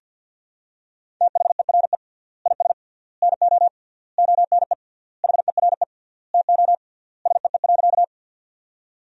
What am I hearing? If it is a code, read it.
Morse code, heard as a keyed tone, about 50 wpm, 700 Hz. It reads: THERE IS NO ONE HERE TO SEE.
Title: 3x Word-Spacing - Sentences from Top 200 Words - 50wpm